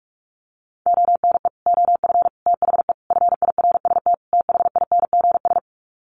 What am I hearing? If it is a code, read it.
Morse code, heard as a keyed tone, about 44 wpm, 700 Hz. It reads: ONE OF THE FIRST THINGS
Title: Sets of 5 Words Encore 45wpm